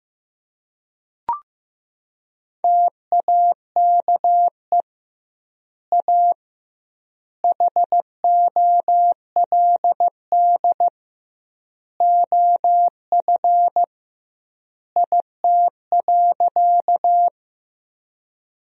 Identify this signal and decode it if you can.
Morse code, heard as a keyed tone, about 15 wpm, 700 Hz. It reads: TAKE A HOLD OF IT.
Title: 2x Word-Spacing - Sentences from Top 400 Words - 15wpm